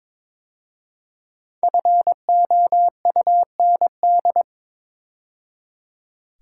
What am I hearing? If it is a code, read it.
Morse code, heard as a keyed tone, about 22 wpm, 700 Hz. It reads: FOUND
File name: 5 Letter Words - First 3 Letters Given - 22wpm